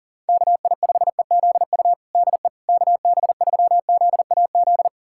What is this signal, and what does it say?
Morse code, heard as a keyed tone, about 40 wpm, 700 Hz. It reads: KI5EZU DE KB3ZAZ
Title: USA Callsign DE Callsign - 6 Characters - 40wpm